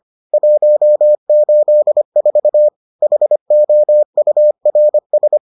Morse code, heard as a keyed tone, about 25 wpm, 600 Hz. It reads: 184 HOURS